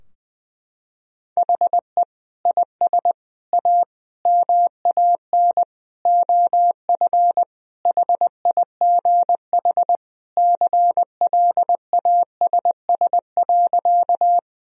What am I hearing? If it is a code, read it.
Morse code, heard as a keyed tone, about 20 wpm, 700 Hz. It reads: HE IS A MAN OF HIGH CLASS.
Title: Sentences from Top 1000 Words - Review Part 2 - 20wpm